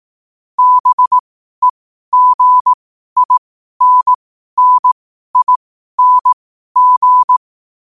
Morse code, heard as a keyed tone, about 18 wpm, spaced as for 12 wpm, 1000 Hz. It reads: BEGINNING
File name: qotdfiles / 12wpm